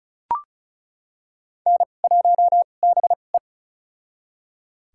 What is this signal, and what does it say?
Morse code, heard as a keyed tone, about 36 wpm, spaced as for 25 wpm, 700 Hz. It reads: N1BE
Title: USA Call Signs - 4 Characters - 2x Character Spacing - 35wpm